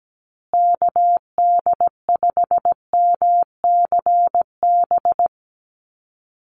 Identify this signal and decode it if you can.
Morse code, heard as a keyed tone, about 17 wpm, 700 Hz. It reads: KD5MCB
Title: USA Call Signs Repeated 3x 17wpm